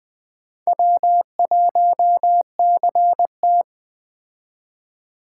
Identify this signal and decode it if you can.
Morse code, heard as a keyed tone, about 20 wpm, 700 Hz. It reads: W1CT